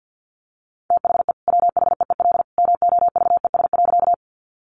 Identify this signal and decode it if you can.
Morse code, heard as a keyed tone, about 50 wpm, 700 Hz. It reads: THE WHEEL MOVES.
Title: Additional Easy Phrases 50wpm